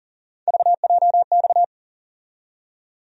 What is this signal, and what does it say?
Morse code, heard as a keyed tone, about 40 wpm, 700 Hz. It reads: VJX
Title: USA Call Sign Postfixes - 3 Letters 40wpm